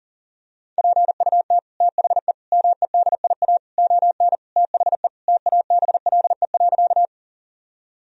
Morse code, heard as a keyed tone, about 40 wpm, 700 Hz. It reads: PUT THE MEDIA ON THE TABLE.
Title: Sentences from Top 1000 Words - Review Part 4 - 40wpm